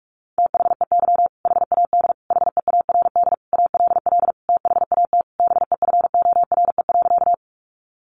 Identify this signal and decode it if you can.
Morse code, heard as a keyed tone, about 44 wpm, 700 Hz. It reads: THEY HAD HEARD ALL THAT BEFORE.
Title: Sentences from Top 400 Words 45wpm